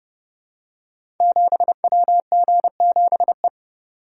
Morse code, heard as a keyed tone, about 30 wpm, 700 Hz. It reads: 7WG7E
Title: Code Groups - 5 Letter:Numbers - 30wpm